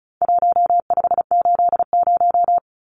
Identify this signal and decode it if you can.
Morse code, heard as a keyed tone, about 34 wpm, 700 Hz. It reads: 1580
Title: Numbers - Contest Style - 4-Digit 35wpm